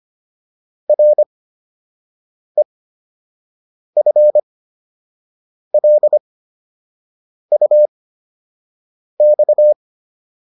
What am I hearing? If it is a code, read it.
Morse code, heard as a keyed tone, about 25 wpm, spaced as for 6 wpm, 600 Hz. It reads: REFLUX